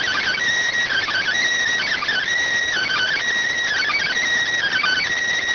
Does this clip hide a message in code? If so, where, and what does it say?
Yes: Emergency Alert System (SAME) tones from 0.9 to 5.0 s, heard as harsh buzzing bursts.